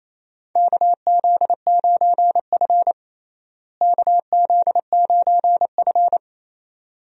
Morse code, heard as a keyed tone, about 28 wpm, 700 Hz. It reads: KZ9F KZ9F
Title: USA Call Signs - 4 Characters - Repeated 2x - 28wpm